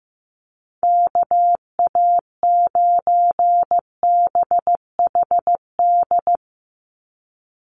Morse code, heard as a keyed tone, about 15 wpm, 700 Hz. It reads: KA9BHD
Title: Call Signs Repeated 2x - 15wpm